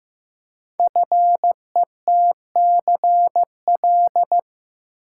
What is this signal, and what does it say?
Morse code, heard as a keyed tone, about 15 wpm, 700 Hz. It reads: FETCL
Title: Code Groups - 5 Letters - 15wpm